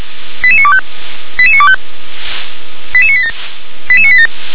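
Five-tone selective call (ZVEI), heard as quick runs of five tones, tones of about 70 ms. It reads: 88925 88925 88987 88987
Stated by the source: ZVEI five-tone selective call received with an SDR.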